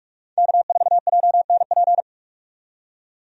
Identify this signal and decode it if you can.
Morse code, heard as a keyed tone, about 46 wpm, 700 Hz. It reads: K4JNP
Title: USA Call Signs - 5 Characters - 45wpm